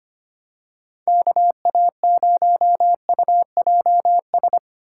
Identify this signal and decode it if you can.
Morse code, heard as a keyed tone, about 25 wpm, 700 Hz. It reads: KA0UJS
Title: USA Call Signs - 6 Characters - 8x Character Spacing - 25wpm